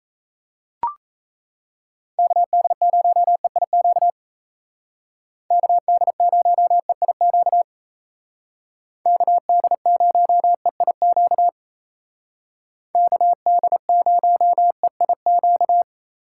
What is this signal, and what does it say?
Morse code, heard as a keyed tone, about 35 wpm, 700 Hz. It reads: KD0EIQ KD0EIQ KD0EIQ KD0EIQ
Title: Speed Racing - USA 2x3 Call Signs 28wpm